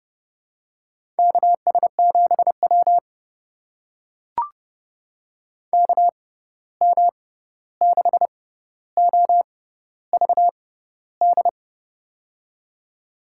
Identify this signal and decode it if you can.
Morse code, heard as a keyed tone, about 31 wpm, spaced as for 14 wpm, 700 Hz. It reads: KS7W KM6OVD
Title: USA Call Signs - 6x Character Spacing - 30wpm